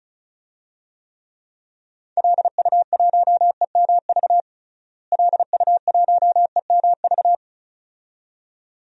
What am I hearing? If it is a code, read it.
Morse code, heard as a keyed tone, about 34 wpm, 700 Hz. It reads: LU1EMV LU1EMV
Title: DXCC-100 Call Signs - 6 Characters - Repeated 2x - 35wpm